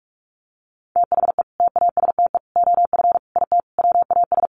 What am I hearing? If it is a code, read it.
Morse code, heard as a keyed tone, about 44 wpm, 700 Hz. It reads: THE TASTE OF IT WAS
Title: Sentences from Top 1100 Words 45wpm